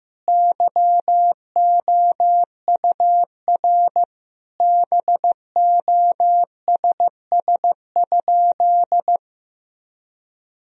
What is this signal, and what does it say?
Morse code, heard as a keyed tone, about 15 wpm, 700 Hz. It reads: YOUR BOSS?
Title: Sentences from Top 1000 Words - Review Part 4 - 15wpm